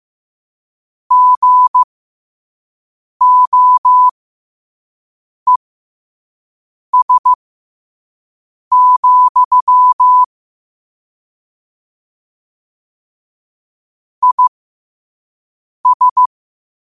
Morse code, heard as a keyed tone, about 15 wpm, spaced as for 5 wpm, 1000 Hz. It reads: GOES, IS
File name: qotdfiles / 5wpm